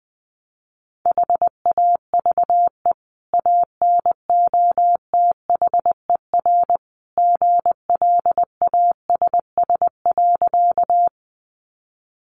Morse code, heard as a keyed tone, about 20 wpm, 700 Hz. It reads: HAVE ANOTHER GLASS.
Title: Sentences from Top 1300 Words 20wpm